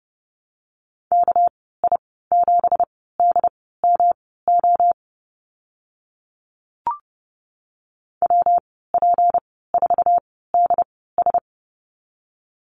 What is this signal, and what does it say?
Morse code, heard as a keyed tone, about 30 wpm, spaced as for 17 wpm, 700 Hz. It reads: KI7DMO WP4DS
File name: Lesson 37g - Letter x - Call Signs 30wpm